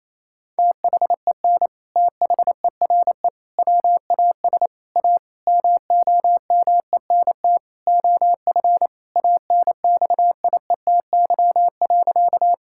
Morse code, heard as a keyed tone, about 28 wpm, 700 Hz. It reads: THEN THERE WAS A MOMENT OF ANXIETY.
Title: Sentences from Top 1600 Words 28wpm